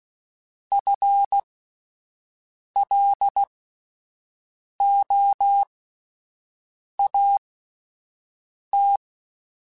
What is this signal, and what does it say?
Morse code, heard as a keyed tone, about 16 wpm, spaced as for 5 wpm, 800 Hz. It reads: FLOAT
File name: Alphabetical / Alphabetical mp3s / Fs @5